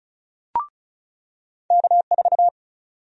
Morse code, heard as a keyed tone, about 36 wpm, 700 Hz. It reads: K4
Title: USA Call Sign Prefixes - 35wpm